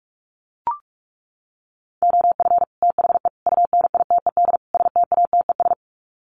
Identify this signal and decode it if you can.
Morse code, heard as a keyed tone, about 44 wpm, 700 Hz. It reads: OF THE UNITED STATES
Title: Sets of 4 Words 45wpm